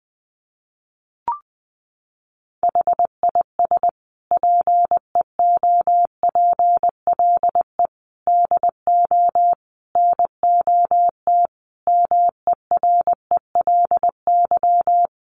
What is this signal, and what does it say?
Morse code, heard as a keyed tone, about 19 wpm, 700 Hz. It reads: HIS PEOPLE DO NOT MERELY
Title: Sentences from Top 1400 Words 20wpm